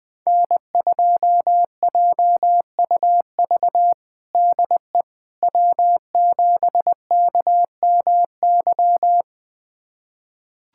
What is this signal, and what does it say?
Morse code, heard as a keyed tone, about 19 wpm, 700 Hz. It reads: N2JUV DE W7KMY
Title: USA Callsign DE Callsign - 5 Characters - 20wpm